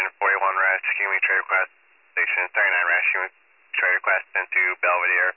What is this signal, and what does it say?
Emergency Alert System (SAME) tones, heard as harsh buzzing bursts, with speech. It is heard from 1.0 to 4.2 s.